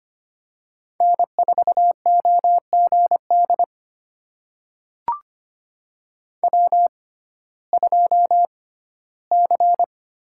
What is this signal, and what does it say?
Morse code, heard as a keyed tone, about 25 wpm, spaced as for 15 wpm, 700 Hz. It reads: N4OGD W2C
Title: USA Call Signs - 5 Characters - 6x Character Spacing - 25wpm